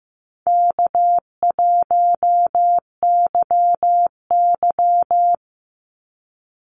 Morse code, heard as a keyed tone, about 15 wpm, 700 Hz. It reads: K1YY
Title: Easy Call Signs Repeated 3x - 15wpm